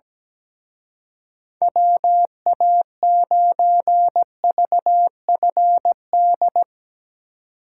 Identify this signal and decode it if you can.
Morse code, heard as a keyed tone, about 18 wpm, 700 Hz. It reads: WA9VFD
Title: Speed Racing - USA 2x3 Call Signs 17wpm